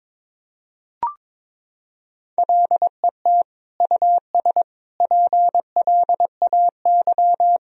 Morse code, heard as a keyed tone, about 21 wpm, 700 Hz. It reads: LET US PLAY
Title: Sentences from Top 1000 Words - Review Part 3 - 22wpm